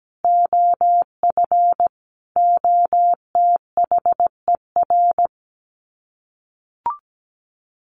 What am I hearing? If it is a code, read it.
Morse code, heard as a keyed tone, about 17 wpm, 700 Hz. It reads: OF OTHER
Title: Sets of 5 Words Encore2 17wpm